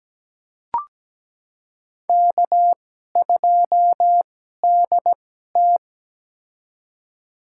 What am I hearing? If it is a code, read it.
Morse code, heard as a keyed tone, about 17 wpm, spaced as for 12 wpm, 700 Hz. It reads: K2DT
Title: USA Call Signs - 4 Characters - 2x Character Spacing - 17wpm